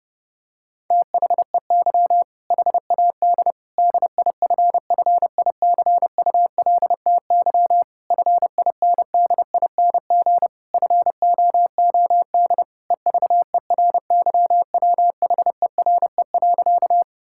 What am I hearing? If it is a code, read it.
Morse code, heard as a keyed tone, about 30 wpm, 700 Hz. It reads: THEY HAD DIFFICULTY FINDING FOOD EVERYWHERE.
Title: Sentences from Top 1600 Words 30wpm